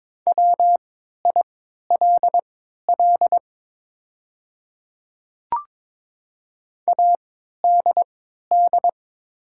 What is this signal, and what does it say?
Morse code, heard as a keyed tone, about 22 wpm, spaced as for 13 wpm, 690 Hz. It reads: WILL ADD
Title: 3x Character Spacing - Top 200 Words - 22wpm